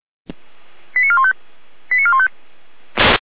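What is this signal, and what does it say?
Five-tone selective call (ZVEI), heard as quick runs of five tones, tones of about 70 ms. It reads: 89415 89415
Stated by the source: ZVEI five-tone selective call received on an SDR.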